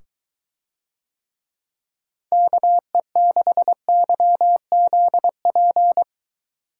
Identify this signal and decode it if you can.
Morse code, heard as a keyed tone, about 23 wpm, 700 Hz. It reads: KE6YZP